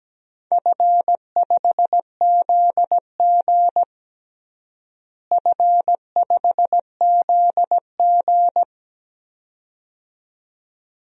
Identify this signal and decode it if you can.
Morse code, heard as a keyed tone, about 17 wpm, 700 Hz. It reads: F5ZG F5ZG
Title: DXCC-100 Call Signs - 4 Characters - Repeated 2x - 17wpm